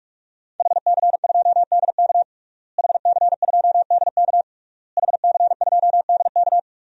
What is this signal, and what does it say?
Morse code, heard as a keyed tone, about 46 wpm, 700 Hz. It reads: HC2DK HC2DK HC2DK
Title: DXCC-100 Call Signs - 5 Characters - Repeated 3x - 45wpm